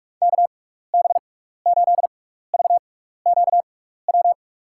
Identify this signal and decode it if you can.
Morse code, heard as a keyed tone, about 46 wpm, spaced as for 15 wpm, 690 Hz. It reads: KB8VQW
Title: USA Call Signs - 6x Character Spacing - 45wpm